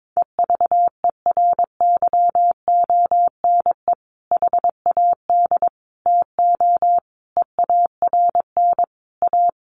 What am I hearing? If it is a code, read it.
Morse code, heard as a keyed tone, about 22 wpm, 700 Hz. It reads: EVERYONE HAD TO EARN A
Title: Sentences from Top 1500 Words 22wpm